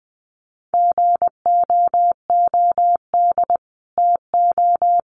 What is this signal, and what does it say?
Morse code, heard as a keyed tone, about 20 wpm, 700 Hz. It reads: GOOD TO
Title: Sets of 4 Words 20wpm